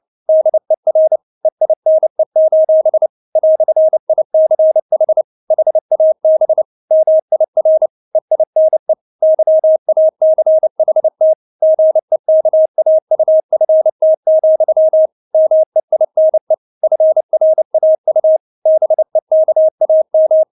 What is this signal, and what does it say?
Morse code, heard as a keyed tone, about 29 wpm, 630 Hz. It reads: DER EINE: "ICH HAB MIR EINE YACHT GEKAUFT, MEINE FRAU BEKAM
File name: Witz10 29wpm